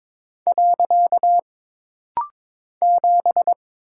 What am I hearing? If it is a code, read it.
Morse code, heard as a keyed tone, about 22 wpm, 700 Hz. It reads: . 7
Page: This rapid-fire timing is geared to push your ICR (Instant Character Recognition) skills.